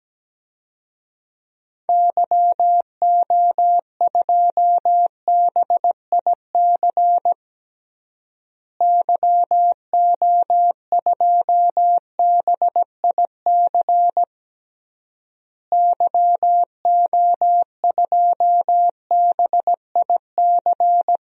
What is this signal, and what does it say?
Morse code, heard as a keyed tone, about 17 wpm, 700 Hz. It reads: YO2BIC YO2BIC YO2BIC
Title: DXCC-100 Call Signs - 6 Characters - Repeated 3x - 17wpm